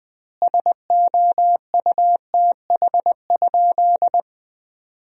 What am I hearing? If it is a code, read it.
Morse code, heard as a keyed tone, about 20 wpm, 700 Hz. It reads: SOUTH?
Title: 2x Word-Spacing - Sentences from Top 400 Words - 20wpm